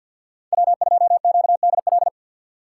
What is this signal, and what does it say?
Morse code, heard as a keyed tone, about 50 wpm, 700 Hz. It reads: W2QDL
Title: USA Call Signs - 5 Characters - 4x Character Spacing - 50wpm